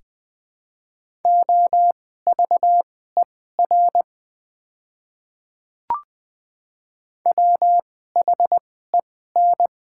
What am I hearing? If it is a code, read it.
Morse code, heard as a keyed tone, about 21 wpm, spaced as for 15 wpm, 700 Hz. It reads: OVER WHEN
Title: 2x Character Spacing - Top 100 Words - 20wpm